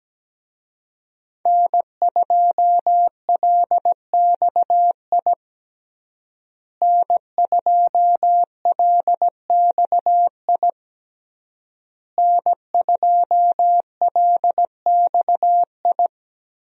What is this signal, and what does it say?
Morse code, heard as a keyed tone, about 17 wpm, 700 Hz. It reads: N2LXI N2LXI N2LXI
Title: USA Call Signs - 5 Characters - Repeated 3x - 17wpm